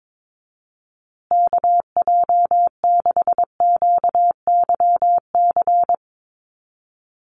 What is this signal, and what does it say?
Morse code, heard as a keyed tone, about 22 wpm, 700 Hz. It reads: KJ6QYC
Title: USA Call Signs Repeated 2x 22wpm